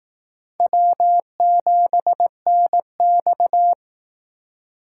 Morse code, heard as a keyed tone, about 18 wpm, 700 Hz. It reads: W7NX
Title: Speed Racing - USA 1x2 Call Signs 15wpm